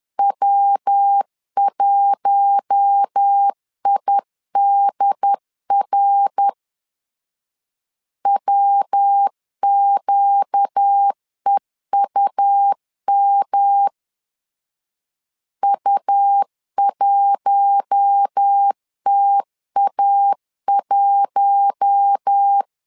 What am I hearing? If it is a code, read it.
Morse code, heard as a keyed tone, about 10 wpm, 800 Hz. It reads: W1IDR WQEUM U1TA1
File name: LEZIONI / 4 Q1 10wpm 2448